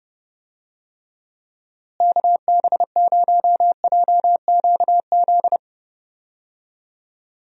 Morse code, heard as a keyed tone, about 32 wpm, 700 Hz. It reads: KB0JQZ
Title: USA Call Signs - 6 Characters - Repeated 3x - 30wpm